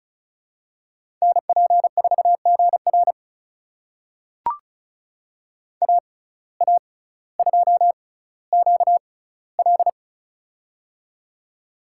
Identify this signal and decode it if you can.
Morse code, heard as a keyed tone, about 34 wpm, spaced as for 18 wpm, 700 Hz. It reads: NP4GR AA2QL
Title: USA Call Signs - 5 Characters - 6x Character Spacing - 35wpm